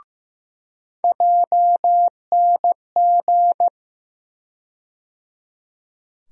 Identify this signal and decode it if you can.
Morse code, heard as a keyed tone, about 15 wpm, 700 Hz. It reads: JNG
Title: USA Call Sign Postfixes - 3 Letters 15wpm